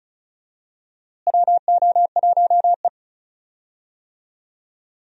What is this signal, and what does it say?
Morse code, heard as a keyed tone, about 34 wpm, 700 Hz. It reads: WO1E
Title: USA Call Signs - 4 Characters - Repeated 2x - 35wpm